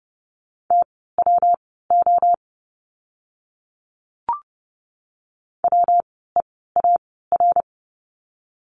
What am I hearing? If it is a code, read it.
Morse code, heard as a keyed tone, about 30 wpm, spaced as for 17 wpm, 700 Hz. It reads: TWO WEAR
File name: course / 30:10wpm / 115 Lesson 19f - Letter w - Words 30wpm